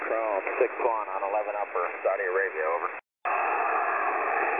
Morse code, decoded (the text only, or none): none